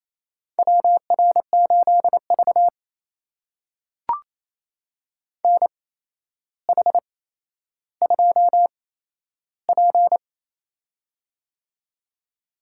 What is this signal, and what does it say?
Morse code, heard as a keyed tone, about 27 wpm, spaced as for 12 wpm, 700 Hz. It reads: WR8V NH2P
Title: USA Call Signs - 4 Characters - 8x Character Spacing - 28wpm